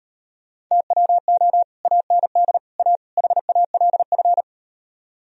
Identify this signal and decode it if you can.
Morse code, heard as a keyed tone, about 38 wpm, 700 Hz. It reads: TWO AND A HALF